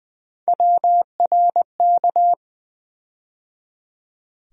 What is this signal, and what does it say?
Morse code, heard as a keyed tone, about 20 wpm, 700 Hz. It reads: WRK
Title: USA Call Sign Postfixes - 3 Letters 20wpm